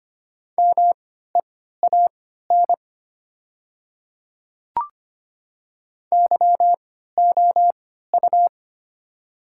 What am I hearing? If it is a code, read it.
Morse code, heard as a keyed tone, about 25 wpm, spaced as for 14 wpm, 700 Hz. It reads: MEAN YOU
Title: 3x Character Spacing - Top 200 Words - 25wpm